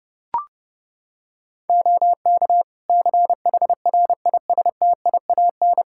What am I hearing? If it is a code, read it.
Morse code, heard as a keyed tone, about 30 wpm, 700 Hz. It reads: OK CHRISTIAN
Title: Practice QSO - Exchange 3 - More Info - 30wpm